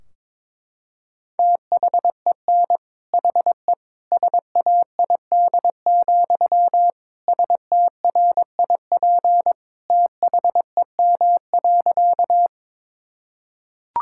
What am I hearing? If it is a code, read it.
Morse code, heard as a keyed tone, about 22 wpm, 700 Hz. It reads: THEN HE SAID, STRIP THEM.
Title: Sentences from Top 1600 Words 22wpm